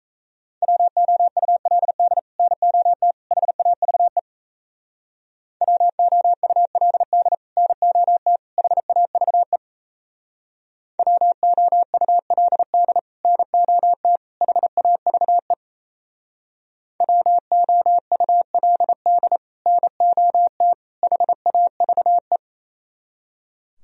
Morse code, heard as a keyed tone, about 34 wpm, 700 Hz. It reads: WOULD NOT HAVE WOULD NOT HAVE WOULD NOT HAVE WOULD NOT HAVE
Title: Speed-Racing - Sets of 3 Words 28wpm